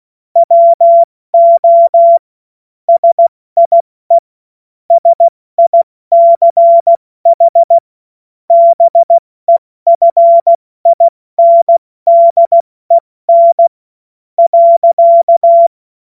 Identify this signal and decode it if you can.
Morse code, heard as a keyed tone, about 16 wpm, spaced as for 14 wpm, 680 Hz. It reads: WO SIE SICH BEFINDEN.
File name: Morsekurs / Witze / Witz 03. Mai 2022 70 F80